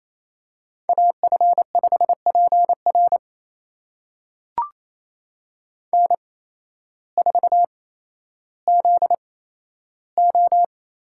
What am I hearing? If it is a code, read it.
Morse code, heard as a keyed tone, about 28 wpm, spaced as for 13 wpm, 700 Hz. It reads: AF5PR N4ZO
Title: USA Call Signs - 5 Characters - 8x Character Spacing - 28wpm